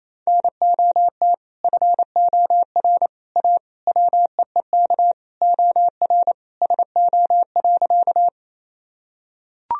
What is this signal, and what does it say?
Morse code, heard as a keyed tone, about 28 wpm, 700 Hz. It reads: NOT FOR A WEEK OR SO.
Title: Sentences from Top 1000 Words - Review Part 2 - 28wpm